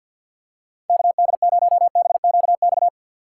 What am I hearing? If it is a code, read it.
Morse code, heard as a keyed tone, about 49 wpm, 700 Hz. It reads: KD0BQX
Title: USA Call Signs - 6 Characters - 4x Character Spacing - 50wpm